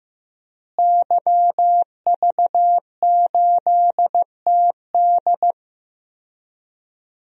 Morse code, heard as a keyed tone, about 15 wpm, 700 Hz. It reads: YV8TD
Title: DXCC-100 Call Signs - 5 Characters - Repeated 2x - 15wpm